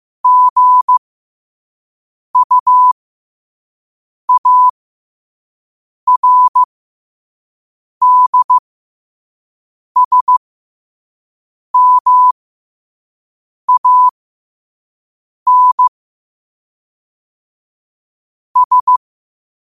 Morse code, heard as a keyed tone, about 15 wpm, spaced as for 5 wpm, 1000 Hz. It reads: GUARDSMANS
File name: qotdfiles / 5wpm